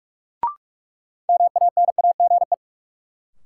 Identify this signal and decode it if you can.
Morse code, heard as a keyed tone, about 46 wpm, 700 Hz. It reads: MANAGE
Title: Sentences from Top 1400 Words - New Word Drill - 45wpm